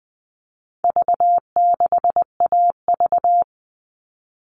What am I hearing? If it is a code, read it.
Morse code, heard as a keyed tone, about 20 wpm, 700 Hz. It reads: V6AV
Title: All Cut Numbers - 4-Digit 20wpm